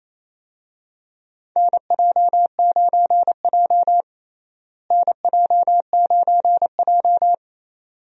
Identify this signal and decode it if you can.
Morse code, heard as a keyed tone, about 27 wpm, 700 Hz. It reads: NJ9J NJ9J
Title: USA Call Signs - 4 Characters - Repeated 2x - 28wpm